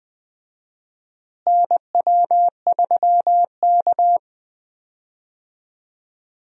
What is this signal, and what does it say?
Morse code, heard as a keyed tone, about 20 wpm, 700 Hz. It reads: NW3K
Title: USA Call Signs - 4 Characters - Repeated 2x - 20wpm